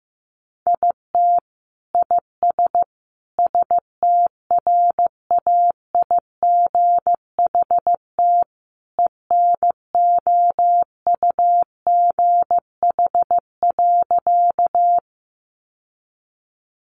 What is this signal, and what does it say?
Morse code, heard as a keyed tone, about 15 wpm, 700 Hz. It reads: IT IS STRAIGHT ENOUGH.
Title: Sentences from Top 1000 Words 15wpm